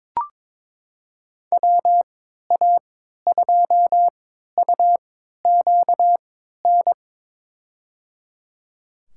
Morse code, heard as a keyed tone, about 22 wpm, spaced as for 13 wpm, 700 Hz. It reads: WA2UQN
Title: USA Call Signs - 6 Characters - 3x Character Spacing - 22wpm